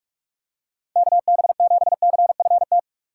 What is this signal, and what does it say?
Morse code, heard as a keyed tone, about 46 wpm, 700 Hz. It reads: KB7CFT